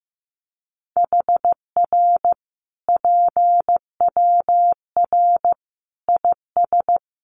Morse code, heard as a keyed tone, about 15 wpm, 700 Hz. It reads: HR PWR IS
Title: Common QSO Phrases 15wpm